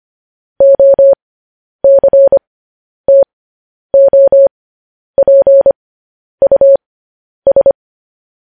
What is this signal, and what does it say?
Morse code, heard as a keyed tone, about 25 wpm, spaced as for 10 wpm, 550 Hz. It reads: OCTOPUS
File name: podcast / News headlines-1x-25x10